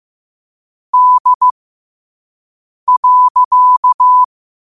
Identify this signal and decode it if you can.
Morse code, heard as a keyed tone, about 15 wpm, spaced as for 5 wpm, 1000 Hz. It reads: D.
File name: qotdfiles / 5wpm